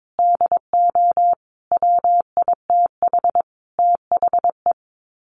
Morse code, heard as a keyed tone, about 22 wpm, 700 Hz. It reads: DO WITH THE
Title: Sets of 5 Words Encore 22wpm